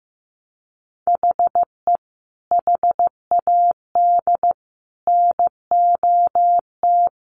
Morse code, heard as a keyed tone, about 15 wpm, 700 Hz. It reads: HE HAD NOT
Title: Sentences from Top 1100 Words 15wpm